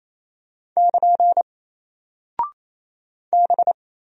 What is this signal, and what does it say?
Morse code, heard as a keyed tone, about 28 wpm, 700 Hz. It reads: <KN> B
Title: Single Character (Letter:Number:Punctuation:Prosign) Alt 28wpm